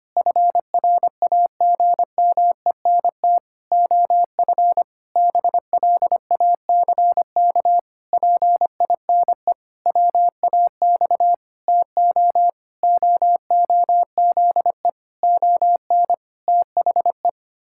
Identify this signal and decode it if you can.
Morse code, heard as a keyed tone, about 25 wpm, 700 Hz. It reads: FRAGMENT OF BLACK PINE WAX TO OOZE ON THE
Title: Pangram Challenge v2 - 25wpm